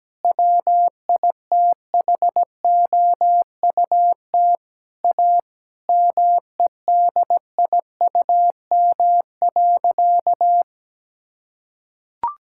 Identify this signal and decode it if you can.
Morse code, heard as a keyed tone, about 17 wpm, 700 Hz. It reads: WITHOUT A MEDIUM.
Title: Sentences from Top 1000 Words - Review Part 6 - 17wpm